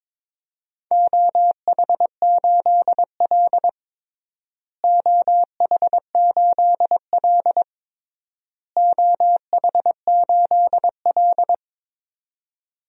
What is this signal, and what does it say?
Morse code, heard as a keyed tone, about 22 wpm, 700 Hz. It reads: OH8L OH8L OH8L
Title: DXCC-100 Call Signs - 4 Characters - Repeated 3x - 22wpm